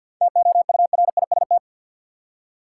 Morse code, heard as a keyed tone, about 47 wpm, 700 Hz. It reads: TOURIST
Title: Seven Letter Words - 50wpm